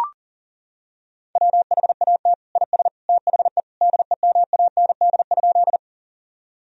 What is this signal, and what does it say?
Morse code, heard as a keyed tone, about 40 wpm, 700 Hz. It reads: WHAT IS THE DEMAND?
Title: Sentences from Top 1000 Words - Review Part 4 - 40wpm